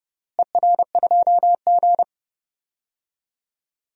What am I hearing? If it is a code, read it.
Morse code, heard as a keyed tone, about 32 wpm, 700 Hz. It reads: ER2G
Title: DXCC-100 Call Signs - 4 Characters - Repeated 3x - 30wpm